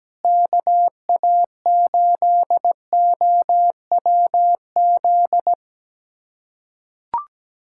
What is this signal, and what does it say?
Morse code, heard as a keyed tone, about 17 wpm, 700 Hz. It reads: KA8OWZ
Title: USA Callsign DE Callsign - 6 Characters - 17wpm